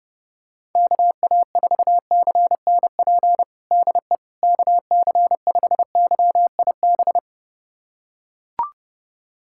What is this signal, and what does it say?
Morse code, heard as a keyed tone, about 30 wpm, 700 Hz. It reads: KA4CNP DE KC5YIB
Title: USA Callsign DE Callsign - 6 Characters - 30wpm